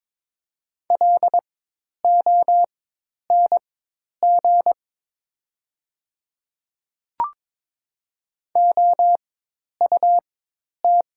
Morse code, heard as a keyed tone, about 22 wpm, spaced as for 10 wpm, 700 Hz. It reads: LONG OUT
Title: 4x Character Spacing - Top 300 Words - 22wpm